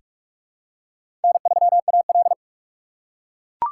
Morse code, heard as a keyed tone, about 42 wpm, 690 Hz. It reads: N3AL
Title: USA Call Signs - 4 Characters - 3x Character Spacing - 45wpm